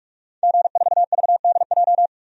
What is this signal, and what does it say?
Morse code, heard as a keyed tone, about 45 wpm, 700 Hz. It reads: G4VDJ
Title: DXCC-100 Call Signs - 5 Characters - 45wpm